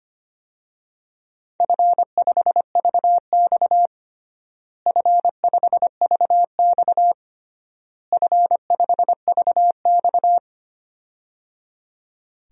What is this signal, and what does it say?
Morse code, heard as a keyed tone, about 25 wpm, 700 Hz. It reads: F5VX F5VX F5VX
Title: DXCC-100 Call Signs - 4 Characters - Repeated 3x - 25wpm